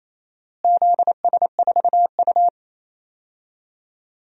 Morse code, heard as a keyed tone, about 28 wpm, 700 Hz. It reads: ZS4U
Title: DXCC-100 Call Signs - 4 Characters - Repeated 2x - 28wpm